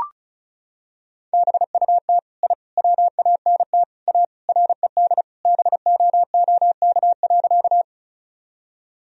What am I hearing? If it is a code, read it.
Morse code, heard as a keyed tone, about 36 wpm, 700 Hz. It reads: BUT I WANT A RED BOOK.
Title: Sentences from Top 1000 Words - Review Part 2 - 35wpm